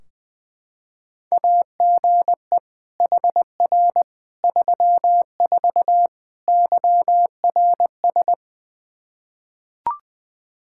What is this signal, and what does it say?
Morse code, heard as a keyed tone, about 20 wpm, 700 Hz. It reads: AGE HR 34 YRS
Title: Practice QSO - Exchange 3 - More Info - 20wpm